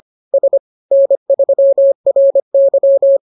Morse code, heard as a keyed tone, about 25 wpm, 550 Hz. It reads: S N3RY